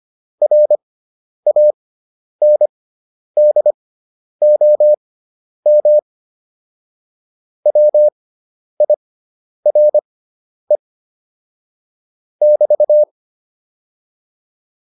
Morse code, heard as a keyed tone, about 25 wpm, spaced as for 10 wpm, 600 Hz. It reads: RANDOM WIRE <BT>